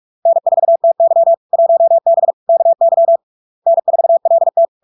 Morse code, heard as a keyed tone, about 46 wpm, 670 Hz. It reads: N4TY 1B KY N4LT
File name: LARK-FD-2022-Log-in-CW-45wpm-670-Hz